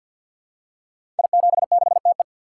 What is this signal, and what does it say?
Morse code, heard as a keyed tone, about 51 wpm, 700 Hz. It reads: I76TE
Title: Code Groups - 5 Letter:Numbers - 50wpm